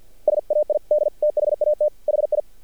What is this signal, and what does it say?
Morse code, heard as a keyed tone, about 48 wpm, 600 Hz. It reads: SAID THAT HI
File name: archived / CRSnet